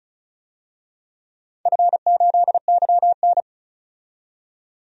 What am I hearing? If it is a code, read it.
Morse code, heard as a keyed tone, about 36 wpm, 700 Hz. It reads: F8YN
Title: DXCC-100 Call Signs - 4 Characters - 35wpm